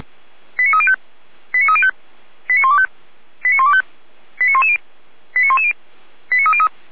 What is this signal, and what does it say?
Five-tone selective call (ZVEI), heard as quick runs of five tones, tones of about 70 ms. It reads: 89305 89305 89125 89125 89110 89110 89303